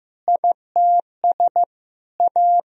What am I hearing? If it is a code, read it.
Morse code, heard as a keyed tone, about 15 wpm, 700 Hz. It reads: ITS A
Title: Sentences from Top 1000 Words - Review Part 4 - 15wpm